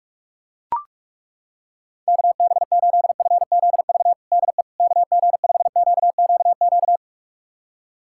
Morse code, heard as a keyed tone, about 41 wpm, 700 Hz. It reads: KB8FZV DE KG5QQQ
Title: USA Callsign DE Callsign - 6 Characters - 45wpm